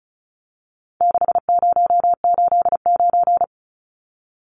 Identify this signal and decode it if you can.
Morse code, heard as a keyed tone, about 34 wpm, 700 Hz. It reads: 6089
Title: Numbers - 4-Digit 35wpm